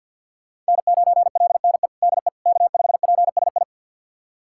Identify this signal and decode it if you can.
Morse code, heard as a keyed tone, about 53 wpm, 690 Hz. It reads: N9LNE DE K5PSI
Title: USA Callsign DE Callsign - 5 Characters - 50wpm